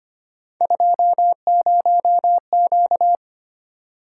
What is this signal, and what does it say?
Morse code, heard as a keyed tone, about 25 wpm, 700 Hz. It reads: 20Q
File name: Three Letter-Number Combinations - 25wpm